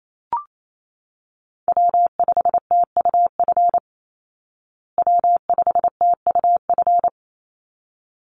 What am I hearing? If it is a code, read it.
Morse code, heard as a keyed tone, about 28 wpm, 700 Hz. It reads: W5TUF W5TUF
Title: Call Signs Repeated 3x - 28wpm